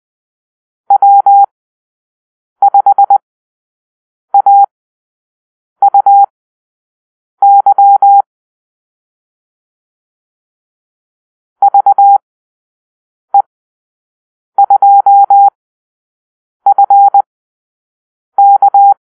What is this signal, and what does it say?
Morse code, heard as a keyed tone, about 20 wpm, spaced as for 6 wpm, 800 Hz. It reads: W5AUY VE2FK
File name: arrl / Call-Sign Drill-6